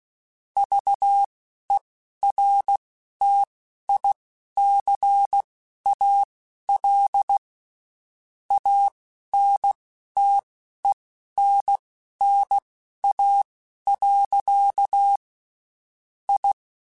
Morse code, heard as a keyed tone, about 16 wpm, spaced as for 11 wpm, 800 Hz. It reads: VERTICAL ANTENNA. I
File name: (spaced) / QSOs Spaced-mp3s / WA7QOX 2 @15 sp